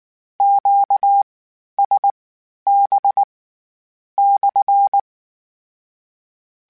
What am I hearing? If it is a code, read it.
Morse code, heard as a keyed tone, about 19 wpm, spaced as for 9 wpm, 800 Hz. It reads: QSB/
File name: Q Calls / Q Calls-mp3s / Q Calls SPSL1 @18